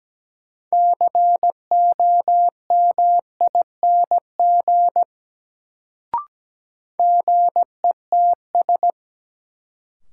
Morse code, heard as a keyed tone, about 17 wpm, 700 Hz. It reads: COMING GETS
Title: Sentences from Top 1300 Words - New Word Drill - 17wpm